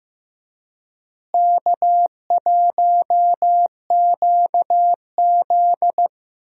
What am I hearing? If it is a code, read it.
Morse code, heard as a keyed tone, about 15 wpm, 700 Hz. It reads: K1QZ
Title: USA Call Signs - 4 Characters - Repeated 3x - 15wpm